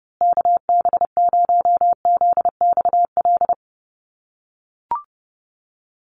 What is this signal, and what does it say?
Morse code, heard as a keyed tone, about 30 wpm, 700 Hz. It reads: KB0ZXL
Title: USA Call Signs Repeated 2x 30wpm